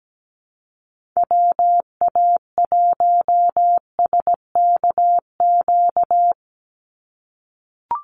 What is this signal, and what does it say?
Morse code, heard as a keyed tone, about 17 wpm, 700 Hz. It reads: WA1SKQ
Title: Call Signs - 17wpm